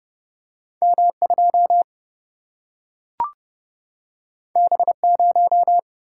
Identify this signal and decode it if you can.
Morse code, heard as a keyed tone, about 30 wpm, 700 Hz. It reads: M 2 B 0
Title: Single-Letter Single-Number 30wpm